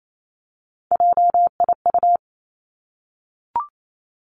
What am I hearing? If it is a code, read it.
Morse code, heard as a keyed tone, about 28 wpm, 700 Hz. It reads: JIU
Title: Three Challenging Letter-Number Combinations 28wpm